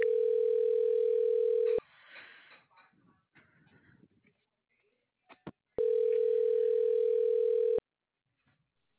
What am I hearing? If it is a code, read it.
Ringback tone (North America), 440 + 480 Hz.